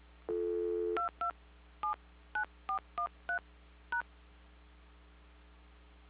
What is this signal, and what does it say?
DTMF, heard as telephone touch tones, tones of about 110 ms. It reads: 22 79413 #